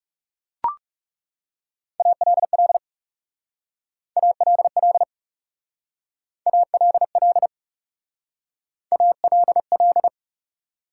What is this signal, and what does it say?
Morse code, heard as a keyed tone, about 38 wpm, 700 Hz. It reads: ALL ALL ALL ALL